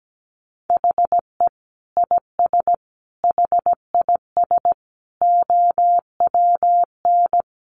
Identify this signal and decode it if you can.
Morse code, heard as a keyed tone, about 17 wpm, 700 Hz. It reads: HE IS HIS OWN
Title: Sentences from Top 1300 Words 17wpm